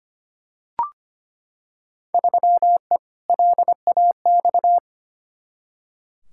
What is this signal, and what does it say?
Morse code, heard as a keyed tone, about 25 wpm, 700 Hz. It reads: 3E LAX
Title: Field Day Exchanges Encore - 25wpm